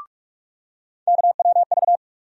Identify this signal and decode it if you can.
Morse code, heard as a keyed tone, about 42 wpm, 700 Hz. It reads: KWV
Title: USA Call Sign Postfixes - 3 Letters 45wpm